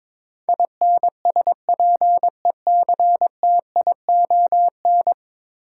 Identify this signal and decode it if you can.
Morse code, heard as a keyed tone, about 22 wpm, 700 Hz. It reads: INSPECTION
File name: practice / 22wpm / 10 Letter Words - First 3 Letters Given - 22wpm